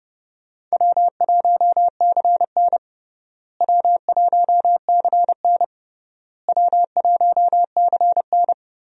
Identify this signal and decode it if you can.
Morse code, heard as a keyed tone, about 32 wpm, 700 Hz. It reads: W1CN W1CN W1CN
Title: USA Call Signs - 4 Characters - Repeated 3x - 30wpm